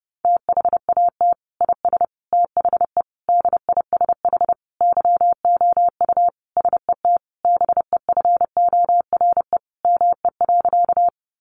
Morse code, heard as a keyed tone, about 28 wpm, 700 Hz. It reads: THAT IS THE DISH YOU SET BEFORE ME.
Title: Sentences from Top 1300 Words 30wpm